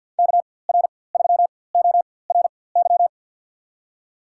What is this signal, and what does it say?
Morse code, heard as a keyed tone, about 52 wpm, spaced as for 23 wpm, 700 Hz. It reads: KR3ORY
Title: USA Call Signs - 6 Characters - 4x Character Spacing - 50wpm